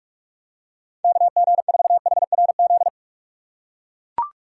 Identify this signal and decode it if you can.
Morse code, heard as a keyed tone, about 44 wpm, 700 Hz. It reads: KG4HRZ